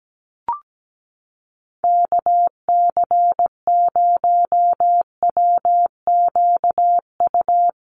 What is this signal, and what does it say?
Morse code, heard as a keyed tone, about 17 wpm, 700 Hz. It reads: KC0WQU
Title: USA Call Signs Repeated 2x 17wpm